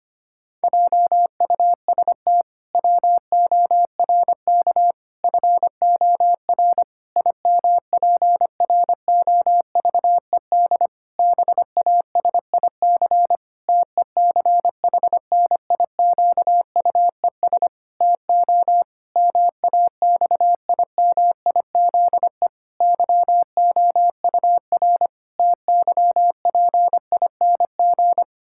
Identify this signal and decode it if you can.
Morse code, heard as a keyed tone, about 25 wpm, 700 Hz. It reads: JUST WORK FOR IMPROVED BASIC TECHNIQUES TO MAXIMIZE YOUR TYPING
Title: Pangram Challenge v2 - 25wpm